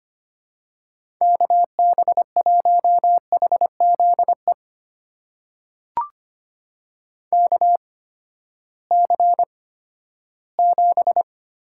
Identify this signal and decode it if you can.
Morse code, heard as a keyed tone, about 25 wpm, spaced as for 14 wpm, 700 Hz. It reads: KB1HZE KC7
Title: USA Call Signs - 6 Characters - 8x Character Spacing - 25wpm